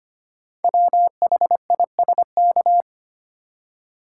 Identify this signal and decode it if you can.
Morse code, heard as a keyed tone, about 25 wpm, 690 Hz. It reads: WHISK
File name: practice / 25wpm / Easy Words with Hints - 25wpm